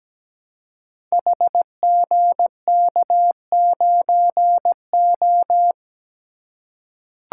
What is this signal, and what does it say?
Morse code, heard as a keyed tone, about 17 wpm, 700 Hz. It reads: HGK9O
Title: Code Groups - 5 Letter:Numbers - 17wpm